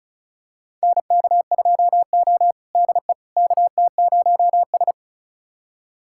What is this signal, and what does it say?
Morse code, heard as a keyed tone, about 36 wpm, 700 Hz. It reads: NK2O DE KT0S
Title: USA Callsign DE Callsign - 4 Characters - 35wpm